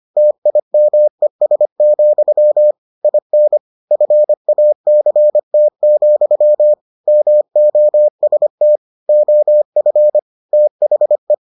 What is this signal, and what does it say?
Morse code, heard as a keyed tone, about 25 wpm, 600 Hz. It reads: TIMES, IN FACT, MOST OF THE